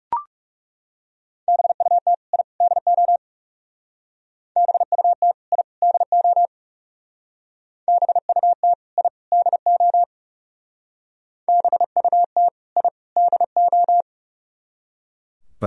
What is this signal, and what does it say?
Morse code, heard as a keyed tone, about 35 wpm, 700 Hz. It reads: BUT I DO BUT I DO BUT I DO BUT I DO
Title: Speed-Racing - Sets of 3 Words 30wpm